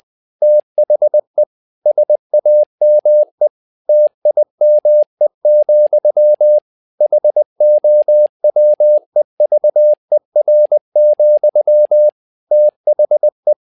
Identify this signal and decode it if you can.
Morse code, heard as a keyed tone, about 20 wpm, 600 Hz. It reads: THE SAME TIME, HOWEVER, THE
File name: learning center / tower trouble / tower trouble 20wpm